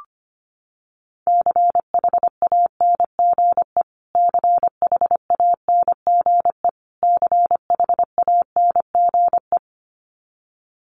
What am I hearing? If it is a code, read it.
Morse code, heard as a keyed tone, about 25 wpm, 700 Hz. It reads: CHANGE CHANGE CHANGE
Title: Top 200 Words Repeated 25wpm